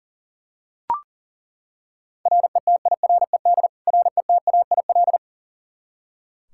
Morse code, heard as a keyed tone, about 40 wpm, 700 Hz. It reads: RETIRED RETAIL
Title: K1USN SST QSO Practice 40wpm